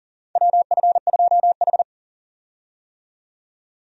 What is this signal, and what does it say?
Morse code, heard as a keyed tone, about 40 wpm, 700 Hz. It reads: WF2H
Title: USA Call Signs - 4 Characters - 40wpm